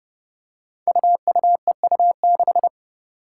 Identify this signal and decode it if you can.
Morse code, heard as a keyed tone, about 30 wpm, 700 Hz. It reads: UUEU6
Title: Five Challenging Letter-Number Combinations 30wpm